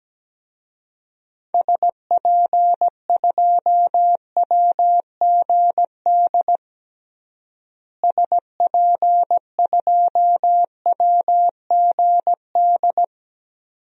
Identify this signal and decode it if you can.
Morse code, heard as a keyed tone, about 17 wpm, 700 Hz. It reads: SP2WGD SP2WGD
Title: DXCC-100 Call Signs - 6 Characters - Repeated 2x - 17wpm